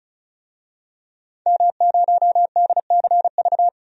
Morse code, heard as a keyed tone, about 36 wpm, 700 Hz. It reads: M0DCV
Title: DXCC-100 Call Signs - 5 Characters - 35wpm